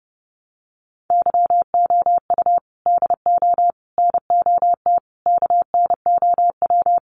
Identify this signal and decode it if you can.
Morse code, heard as a keyed tone, about 30 wpm, 700 Hz. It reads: YOU DO NOT KNOW
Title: Sets of 4 Words 30wpm